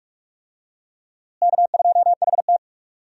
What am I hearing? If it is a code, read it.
Morse code, heard as a keyed tone, about 43 wpm, 700 Hz. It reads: K2HT
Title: USA Call Signs - 4 Characters - Repeated 3x - 45wpm